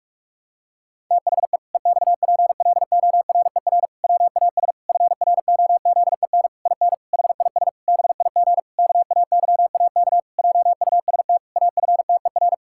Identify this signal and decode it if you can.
Morse code, heard as a keyed tone, about 42 wpm, 700 Hz. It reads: THE EXPLORER WAS FROZEN IN HIS BIG KAYAK JUST AFTER
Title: Pangram Challenge v2 - 45wpm